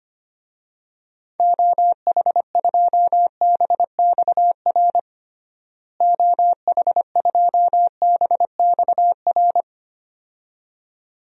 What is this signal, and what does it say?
Morse code, heard as a keyed tone, about 25 wpm, 700 Hz. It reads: OH2BXR OH2BXR
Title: DXCC-100 Call Signs - 6 Characters - Repeated 2x - 25wpm